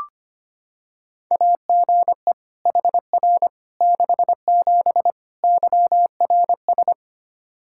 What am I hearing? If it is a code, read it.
Morse code, heard as a keyed tone, about 25 wpm, 700 Hz. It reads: AGE HR 67 YRS
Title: K1USN SST QSO Practice 25wpm